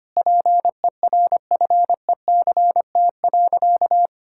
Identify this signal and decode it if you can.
Morse code, heard as a keyed tone, about 25 wpm, 700 Hz. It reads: PERFECT.
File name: practice / 25wpm / Sentences with 3-Word Phrases - 25wpm